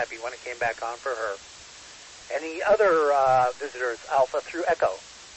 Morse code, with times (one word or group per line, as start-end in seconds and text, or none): none